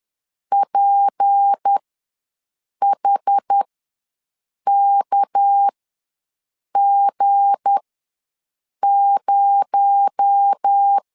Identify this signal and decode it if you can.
Morse code, heard as a keyed tone, about 10 wpm, spaced as for 6 wpm, 800 Hz. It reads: PHKG0
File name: LEZIONI / 7 GF9Z / 7 G ws 10wpm 3843